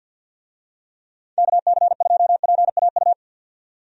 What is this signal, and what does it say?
Morse code, heard as a keyed tone, about 46 wpm, 700 Hz. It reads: KC2PAU
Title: USA Call Signs - 6 Characters - 50wpm